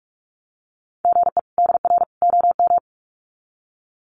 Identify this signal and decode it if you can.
Morse code, heard as a keyed tone, about 44 wpm, 700 Hz. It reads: GE DR OM
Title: Common QSO Phrases 45wpm